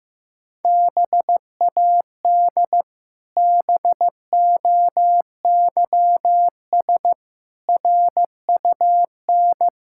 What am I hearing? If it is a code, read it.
Morse code, heard as a keyed tone, about 15 wpm, 700 Hz. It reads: BAD BOYS RUN
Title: Pangram Challenge v2 - 15wpm